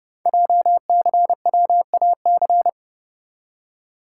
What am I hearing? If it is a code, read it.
Morse code, heard as a keyed tone, about 30 wpm, 700 Hz. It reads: JCWAC
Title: Code Groups - 5 Letters - 30wpm